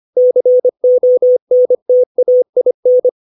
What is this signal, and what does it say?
Morse code, heard as a keyed tone, about 25 wpm, 500 Hz. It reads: CONTAIN
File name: 632 words / 632 Words 25WPM